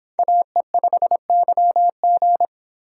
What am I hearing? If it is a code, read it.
Morse code, heard as a keyed tone, about 28 wpm, 700 Hz. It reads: AE5YG